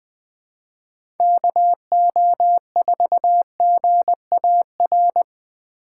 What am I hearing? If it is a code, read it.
Morse code, heard as a keyed tone, about 20 wpm, 700 Hz. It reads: KO4GAR
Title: 2023 POTA Hunter Call Signs - 20wpm